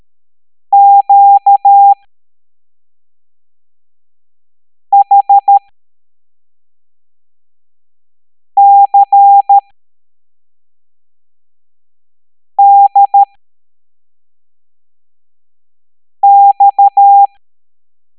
Morse code, heard as a keyed tone, about 13 wpm, spaced as for 3 wpm, 800 Hz. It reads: QHCDX
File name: hamdata / L07-EX03-HQP